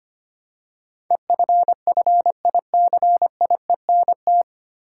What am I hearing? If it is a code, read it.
Morse code, heard as a keyed tone, about 25 wpm, 700 Hz. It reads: EFFICIENT
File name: practice / 25wpm / 9 Letter Words - First 3 Letters Given - 25wpm